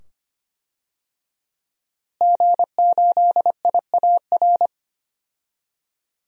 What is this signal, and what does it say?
Morse code, heard as a keyed tone, about 25 wpm, 700 Hz. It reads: G8IAR
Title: DXCC-100 Call Signs - 5 Characters - Repeated 2x - 25wpm